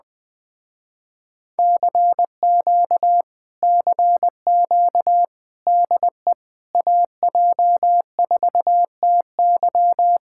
Morse code, heard as a keyed tone, about 20 wpm, 700 Hz. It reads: CQ CQ DE AJ4TY
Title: Speed-Racing Common QSO Phrases Encore - 17wpm